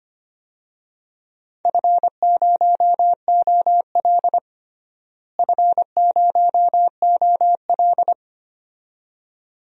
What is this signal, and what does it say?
Morse code, heard as a keyed tone, about 25 wpm, 700 Hz. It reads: F0OL F0OL
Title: DXCC-100 Call Signs - 4 Characters - Repeated 2x - 25wpm